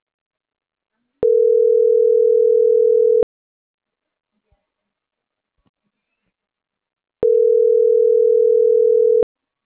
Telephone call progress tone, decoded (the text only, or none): ringback tone